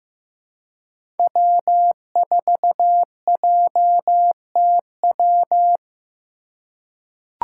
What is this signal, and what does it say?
Morse code, heard as a keyed tone, about 15 wpm, 700 Hz. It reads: W4JTW
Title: USA Call Signs - 2x Character Spacing - 15wpm